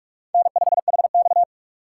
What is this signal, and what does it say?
Morse code, heard as a keyed tone, about 40 wpm, 700 Hz. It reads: N5HX